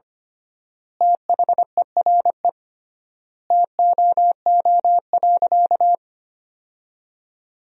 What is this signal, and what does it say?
Morse code, heard as a keyed tone, about 25 wpm, 700 Hz. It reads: THERE TOO.
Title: 3x Word-Spacing - Sentences from Top 200 Words - 25wpm